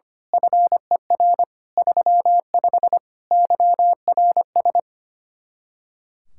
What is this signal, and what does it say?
Morse code, heard as a keyed tone, about 24 wpm, 700 Hz. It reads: FER 35 YRS
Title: Speed-Racing Common QSO Phrases Encore - 25wpm